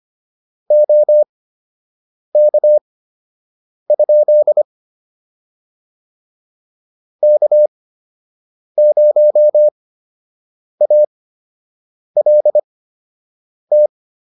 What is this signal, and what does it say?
Morse code, heard as a keyed tone, about 25 wpm, spaced as for 7 wpm, 600 Hz. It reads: OK? K0ALT